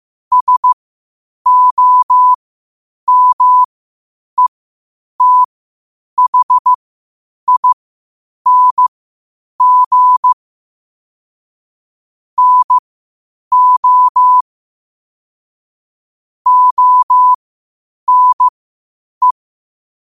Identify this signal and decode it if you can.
Morse code, heard as a keyed tone, about 15 wpm, spaced as for 8 wpm, 1000 Hz. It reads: SOMETHING NO ONE
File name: qotdfiles / 8wpm